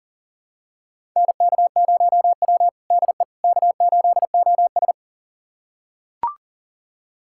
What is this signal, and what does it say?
Morse code, heard as a keyed tone, about 37 wpm, 700 Hz. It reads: NK0W DE K8OS
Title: USA Callsign DE Callsign - 4 Characters - 40wpm